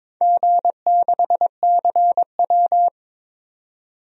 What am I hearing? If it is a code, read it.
Morse code, heard as a keyed tone, about 22 wpm, 700 Hz. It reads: G6CW
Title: DXCC-100 Call Signs - 4 Characters - 22wpm